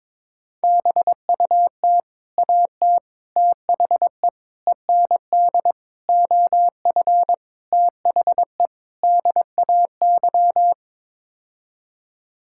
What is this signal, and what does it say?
Morse code, heard as a keyed tone, about 21 wpm, 700 Hz. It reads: BUT AT THE END OF THE DAY
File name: practice / 22wpm / Sets of 7 Words - 22wpm